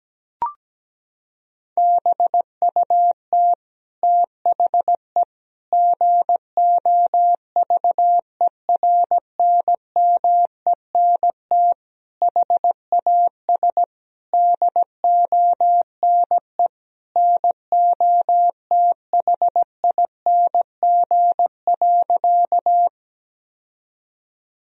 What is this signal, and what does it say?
Morse code, heard as a keyed tone, about 17 wpm, 700 Hz. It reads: BUT THE GOVERNMENT HAS DONE NOTHING.
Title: Sentences from Top 1000 Words - Review Part 3 - 17wpm